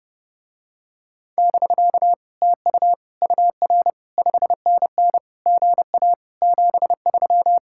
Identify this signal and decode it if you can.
Morse code, heard as a keyed tone, about 32 wpm, 700 Hz. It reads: <BK> TU UR 5NN GA 73